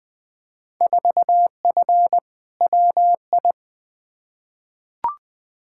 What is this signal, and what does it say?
Morse code, heard as a keyed tone, about 21 wpm, 700 Hz. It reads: 4F WI
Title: Field Day Exchanges Encore - 20wpm